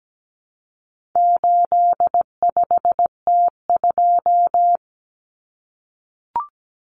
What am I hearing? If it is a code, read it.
Morse code, heard as a keyed tone, about 17 wpm, 700 Hz. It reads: 85T2
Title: Common Cut Numbers - 4-Digit 17wpm